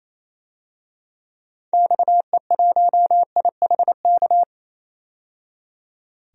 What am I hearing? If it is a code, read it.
Morse code, heard as a keyed tone, about 28 wpm, 700 Hz. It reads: XE1IHK
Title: DXCC-100 Call Signs - 6 Characters - 28wpm